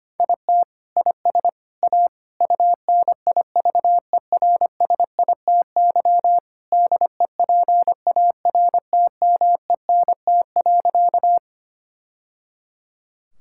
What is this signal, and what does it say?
Morse code, heard as a keyed tone, about 25 wpm, 700 Hz. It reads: IT IS A UNIVERSITY DEPARTMENT.
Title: Sentences from Top 1000 Words - Review Part 5 - 25wpm